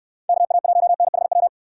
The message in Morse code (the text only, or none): KA1NHW